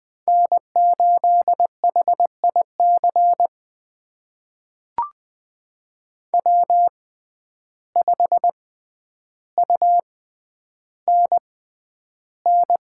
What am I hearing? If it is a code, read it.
Morse code, heard as a keyed tone, about 20 wpm, spaced as for 10 wpm, 700 Hz. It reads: N8HIC W5UNN
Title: USA Call Signs - 5 Characters - 6x Character Spacing - 20wpm